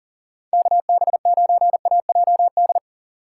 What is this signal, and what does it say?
Morse code, heard as a keyed tone, about 40 wpm, 700 Hz. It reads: KB9AJD
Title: USA Call Signs - 3x Character Spacing - 40wpm